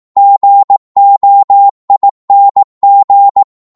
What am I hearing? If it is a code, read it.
Morse code, heard as a keyed tone, about 18 wpm, 800 Hz. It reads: GOING
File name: 18wpm